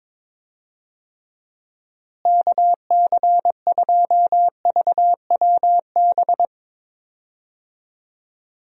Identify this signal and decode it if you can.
Morse code, heard as a keyed tone, about 22 wpm, 700 Hz. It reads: KC2VWB
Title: USA Call Signs - 6 Characters - Repeated 3x - 22wpm